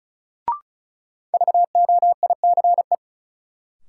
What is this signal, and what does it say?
Morse code, heard as a keyed tone, about 36 wpm, 700 Hz. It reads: VOICE
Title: Sentences from Top 500 Words - New Word Drill - 35wpm